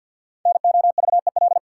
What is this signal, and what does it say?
Morse code, heard as a keyed tone, about 52 wpm, 700 Hz. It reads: NOVEL
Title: Sentences from Top 1300 Words - New Word Drill Alt - 50wpm